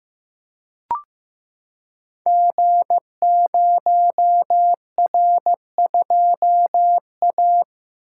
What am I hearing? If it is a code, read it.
Morse code, heard as a keyed tone, about 15 wpm, 700 Hz. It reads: G0R2A
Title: Code Groups - 5 Letter:Numbers - 15wpm